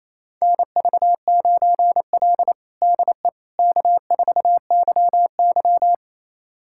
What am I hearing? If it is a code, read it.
Morse code, heard as a keyed tone, about 28 wpm, 700 Hz. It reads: NV9L DE K4YY
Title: USA Callsign DE Callsign - 4 Characters - 28wpm